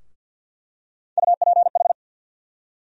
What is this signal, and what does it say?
Morse code, heard as a keyed tone, about 48 wpm, 700 Hz. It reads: UPH
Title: USA Call Sign Postfixes - 3 Letters 50wpm